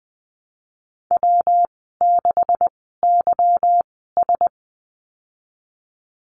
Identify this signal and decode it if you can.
Morse code, heard as a keyed tone, about 19 wpm, spaced as for 14 wpm, 700 Hz. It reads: W6YS
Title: 161 Lesson 26g - Letter g - Call Signs 20wpm